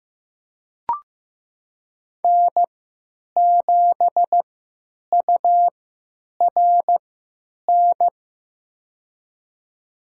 Morse code, heard as a keyed tone, about 15 wpm, spaced as for 8 wpm, 700 Hz. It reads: N7URN
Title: USA Call Signs - 5 Characters - 3x Character Spacing - 15wpm